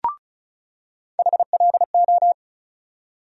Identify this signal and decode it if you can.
Morse code, heard as a keyed tone, about 35 wpm, 700 Hz. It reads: HLO
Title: USA Call Sign Postfixes - 3 Letters 35wpm